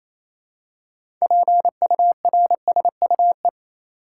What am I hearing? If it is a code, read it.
Morse code, heard as a keyed tone, about 28 wpm, 700 Hz. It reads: PURSUE
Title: Six Letter Words - 28wpm